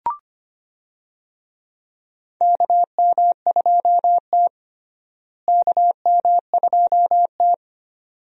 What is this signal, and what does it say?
Morse code, heard as a keyed tone, about 25 wpm, 700 Hz. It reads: KM2T KM2T
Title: USA Call Signs - 4 Characters - Repeated 2x - 25wpm